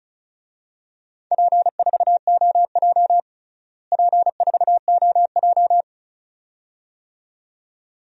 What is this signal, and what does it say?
Morse code, heard as a keyed tone, about 34 wpm, 700 Hz. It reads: P4OJ P4OJ
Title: DXCC-100 Call Signs - 4 Characters - Repeated 2x - 35wpm